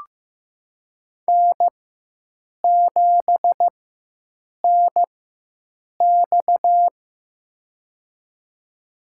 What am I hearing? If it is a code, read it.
Morse code, heard as a keyed tone, about 15 wpm, spaced as for 7 wpm, 700 Hz. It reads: N7NX
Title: USA Call Signs - 4 Characters - 4x Character Spacing - 15wpm